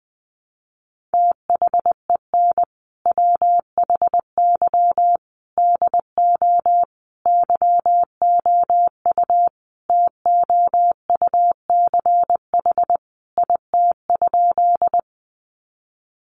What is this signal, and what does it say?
Morse code, heard as a keyed tone, about 20 wpm, 700 Hz. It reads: THEN WHY DO YOU TOUCH IT?
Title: Sentences from Top 1200 Words 20wpm